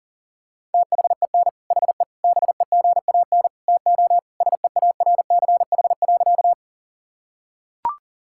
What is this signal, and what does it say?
Morse code, heard as a keyed tone, about 37 wpm, 700 Hz. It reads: THEN HE BEGAN TO SEARCH.
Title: Sentences from Top 1000 Words - Review Part 5 - 40wpm